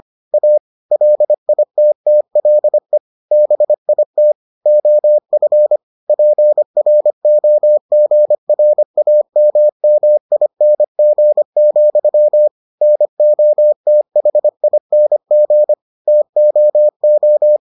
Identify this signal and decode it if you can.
Morse code, heard as a keyed tone, about 25 wpm, 600 Hz. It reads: A LITTLE BIT OF PROGRAMMING, NOTHING TOO